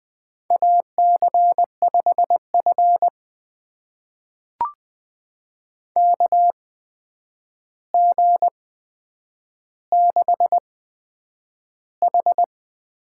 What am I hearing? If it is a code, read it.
Morse code, heard as a keyed tone, about 20 wpm, spaced as for 9 wpm, 700 Hz. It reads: AC5F KG6H
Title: USA Call Signs - 8x Character Spacing - 20wpm